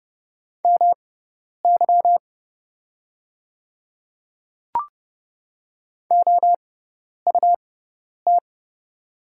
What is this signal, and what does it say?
Morse code, heard as a keyed tone, about 30 wpm, spaced as for 10 wpm, 700 Hz. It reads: MY OUT
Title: 6x Character Spacing - Top 100 Words - 30wpm